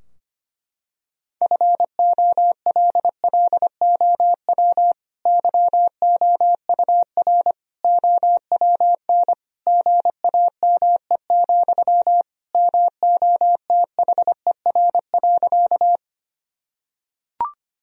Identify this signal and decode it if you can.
Morse code, heard as a keyed tone, about 25 wpm, 700 Hz. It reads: FOLLOW YOUR OWN GAME, MOTHER.
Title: Sentences from Top 1000 Words - Review Part 3 - 25wpm